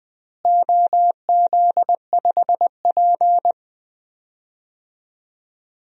Morse code, heard as a keyed tone, about 20 wpm, 700 Hz. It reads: OZ5P
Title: DXCC-100 Call Signs - 4 Characters - Repeated 2x - 20wpm